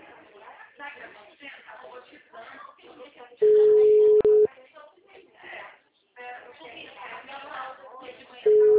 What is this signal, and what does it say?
Ringback tone (Europe, 425 Hz), with speech.